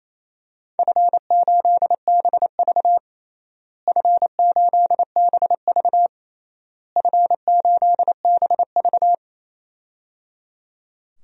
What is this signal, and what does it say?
Morse code, heard as a keyed tone, about 28 wpm, 700 Hz. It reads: F8BV F8BV F8BV
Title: DXCC-100 Call Signs - 4 Characters - Repeated 3x - 28wpm